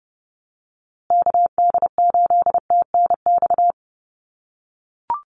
Morse code, heard as a keyed tone, about 30 wpm, 700 Hz. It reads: KD8TNX
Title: USA Call Signs Repeated 3x 30wpm